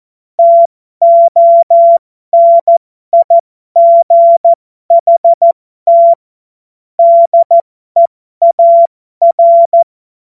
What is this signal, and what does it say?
Morse code, heard as a keyed tone, about 14 wpm, spaced as for 12 wpm, 680 Hz. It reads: TONIGHT DEAR
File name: Morsekurs / Witze